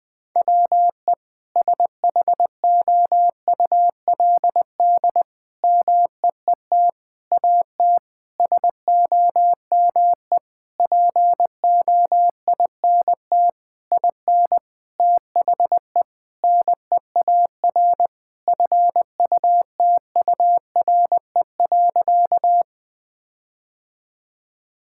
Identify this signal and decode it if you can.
Morse code, heard as a keyed tone, about 20 wpm, 700 Hz. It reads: WE SHOULD MEET AT SOME POINT IN THE NEAR FUTURE.
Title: Sentences with 7-Word Phrases - 20wpm